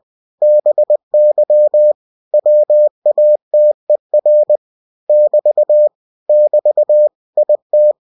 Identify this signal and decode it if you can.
Morse code, heard as a keyed tone, about 20 wpm, 600 Hz. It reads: BY WATER <BT> <BT> IT